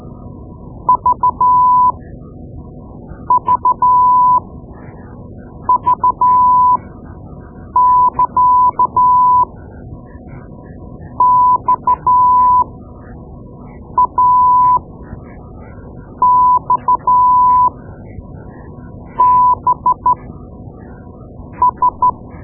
Morse code, heard as a keyed tone, about 12 wpm, spaced as for 5 wpm, 1000 Hz. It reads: VVV <KA> XAXBS
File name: Morse Lessons / Morse Lessons / Tape-2B